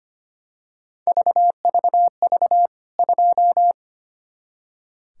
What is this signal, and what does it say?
Morse code, heard as a keyed tone, about 25 wpm, 700 Hz. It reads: VVV 2
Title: K1USN SST QSO Practice 25wpm